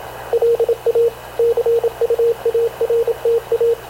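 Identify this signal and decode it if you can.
Morse code, heard as a keyed tone, about 27 wpm, 460 Hz, with radio noise. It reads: LA CUARTA
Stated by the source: Havana Radio / CLA sending Spanish press (PX) in Morse, recorded by Radio Officer Paul Dumesnil onboard CNAV QUEST/CZDO.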